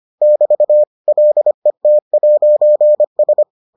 Morse code, heard as a keyed tone, about 25 wpm, 600 Hz. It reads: <BT> LET'S